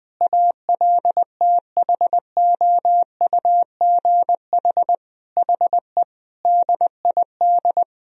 Morse code, heard as a keyed tone, about 21 wpm, 700 Hz. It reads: ALTHOUGH HE DID
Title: Sentences from Top 1000 Words - Review Part 3 - 20wpm